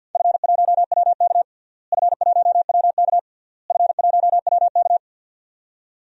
Morse code, heard as a keyed tone, about 53 wpm, 700 Hz. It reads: F1WK F1WK F1WK
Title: DXCC-100 Call Signs - 4 Characters - Repeated 3x - 50wpm